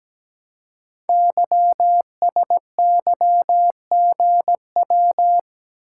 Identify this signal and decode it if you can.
Morse code, heard as a keyed tone, about 17 wpm, 700 Hz. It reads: YSYGW
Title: Code Groups - 5 Letters - 17wpm